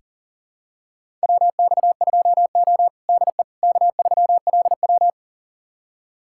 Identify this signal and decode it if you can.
Morse code, heard as a keyed tone, about 40 wpm, 700 Hz. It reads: WX2O DE K3LW
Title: USA Callsign DE Callsign - 4 Characters - 40wpm